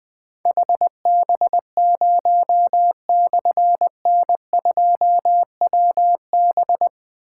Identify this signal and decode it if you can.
Morse code, heard as a keyed tone, about 20 wpm, 700 Hz. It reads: HB0/N2WB
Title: Difficult Call Signs 20wpm